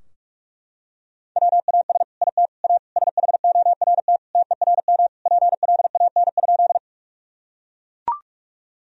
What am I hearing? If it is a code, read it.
Morse code, heard as a keyed tone, about 41 wpm, 700 Hz. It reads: WAS IT A SHORT TERM PLAN?
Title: Sentences from Top 1000 Words - Review Part 4 - 45wpm